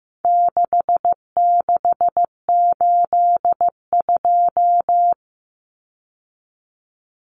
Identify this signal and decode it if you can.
Morse code, heard as a keyed tone, about 15 wpm, 700 Hz. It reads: 6682
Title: Numbers - 4-Digit 15wpm